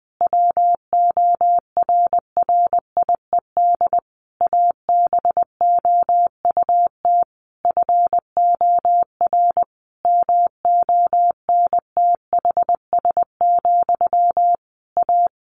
Morse code, heard as a keyed tone, about 20 wpm, 700 Hz. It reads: WORRIED ABOUT FOR MONTHS, A
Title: A Sound Of Thunder 20wpm